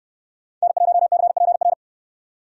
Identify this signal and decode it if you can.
Morse code, heard as a keyed tone, about 69 wpm, 700 Hz. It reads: N1CPR